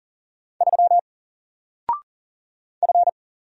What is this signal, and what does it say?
Morse code, heard as a keyed tone, about 40 wpm, 700 Hz. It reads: 3 F
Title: Single Letter-Number Alt 40wpm